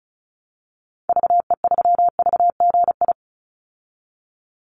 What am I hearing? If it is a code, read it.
Morse code, heard as a keyed tone, about 34 wpm, 700 Hz. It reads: VE3VGI
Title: Call Signs Repeated 2x - 35wpm